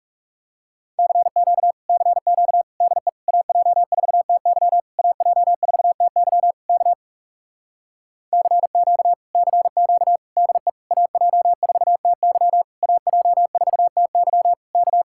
Morse code, heard as a keyed tone, about 40 wpm, 700 Hz. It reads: CQ CQ DE AJ4TY AJ4TY K CQ CQ DE AJ4TY AJ4TY K